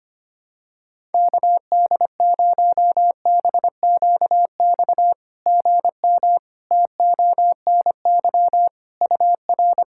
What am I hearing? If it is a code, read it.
Morse code, heard as a keyed tone, about 25 wpm, 690 Hz. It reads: KD0BQX GM TONY UR
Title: SOTA QSO Practice 25wpm